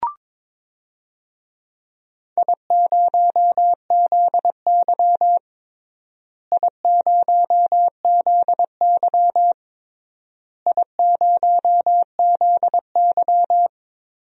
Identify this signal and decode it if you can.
Morse code, heard as a keyed tone, about 21 wpm, 700 Hz. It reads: I0ZY I0ZY I0ZY
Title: DXCC-100 Call Signs - 4 Characters - Repeated 3x - 22wpm